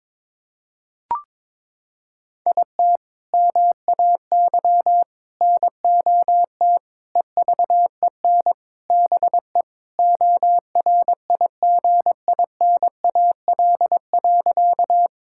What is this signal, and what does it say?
Morse code, heard as a keyed tone, about 22 wpm, 700 Hz. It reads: IT MAY NOT EVEN BE ORIGINAL.
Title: Sentences from Top 1000 Words - Review Part 6 - 22wpm